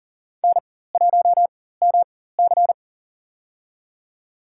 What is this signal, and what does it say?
Morse code, heard as a keyed tone, about 40 wpm, spaced as for 19 wpm, 700 Hz. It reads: N1MC
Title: USA Call Signs - 4x Character Spacing - 40wpm